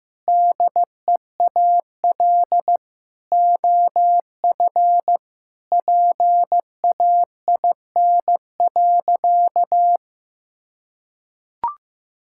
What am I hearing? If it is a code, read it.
Morse code, heard as a keyed tone, about 16 wpm, 700 Hz. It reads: DEAL OF PAIN.
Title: Sentences from Top 1600 Words 15wpm